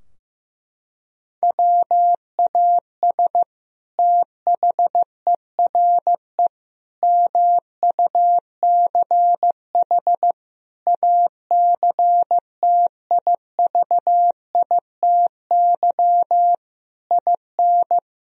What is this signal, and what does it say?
Morse code, heard as a keyed tone, about 15 wpm, 700 Hz. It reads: WAS THERE MUCH ACTIVITY IN
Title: Sentences from Top 1000 Words - Review Part 4 - 15wpm